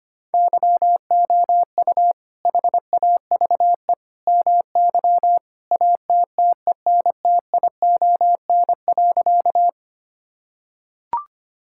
Morse code, heard as a keyed tone, about 25 wpm, 700 Hz. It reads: YOU HAVE MY ATTENTION.
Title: Sentences from Top 1000 Words - Review Part 6 - 25wpm